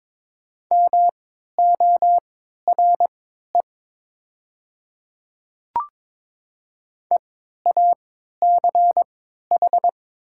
Morse code, heard as a keyed tone, about 21 wpm, spaced as for 12 wpm, 700 Hz. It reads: MORE EACH
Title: 3x Character Spacing - Top 100 Words - 22wpm